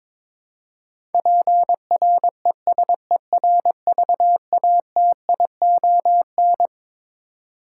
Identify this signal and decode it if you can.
Morse code, heard as a keyed tone, about 21 wpm, 700 Hz. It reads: PRESERVATION
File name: Twelve Letter Words - 22wpm